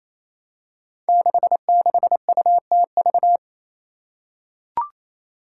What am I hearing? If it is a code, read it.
Morse code, heard as a keyed tone, about 28 wpm, 700 Hz. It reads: 66UTV
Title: Five Challenging Letter-Number Combinations 28wpm